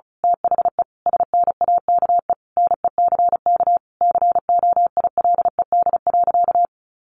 Morse code, heard as a keyed tone, about 34 wpm, 700 Hz. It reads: THE SNAKE NECK COILED.
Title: A Sound Of Thunder 35wpm